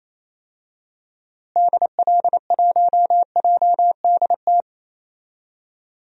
Morse code, heard as a keyed tone, about 28 wpm, 700 Hz. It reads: DL1JDT
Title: DXCC-100 Call Signs - 6 Characters - Repeated 3x - 28wpm